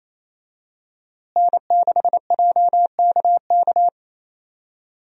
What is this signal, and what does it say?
Morse code, heard as a keyed tone, about 27 wpm, 700 Hz. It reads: N6JKK
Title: USA Call Signs - 5 Characters - Repeated 3x - 28wpm